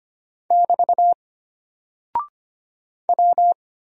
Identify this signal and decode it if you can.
Morse code, heard as a keyed tone, about 25 wpm, 700 Hz. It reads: <BT> W
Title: Single Character (Letter:Number:Punctuation:Prosign) Alt 25wpm